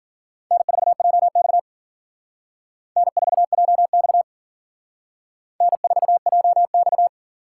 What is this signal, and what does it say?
Morse code, heard as a keyed tone, about 47 wpm, 700 Hz. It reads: N4JX N4JX N4JX
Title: Speed Racing - USA 1x2 Call Signs 40wpm